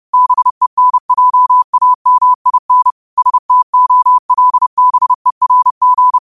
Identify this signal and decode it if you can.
Morse code, heard as a keyed tone, about 30 wpm, 1000 Hz. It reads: BENJAMIN STOLBERG
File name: qotdfiles / 30wpm